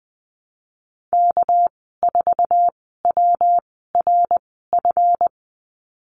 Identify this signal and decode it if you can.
Morse code, heard as a keyed tone, about 20 wpm, spaced as for 15 wpm, 700 Hz. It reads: K4WRF
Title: 200 Lesson 32g - Letter k - Call Signs 20wpm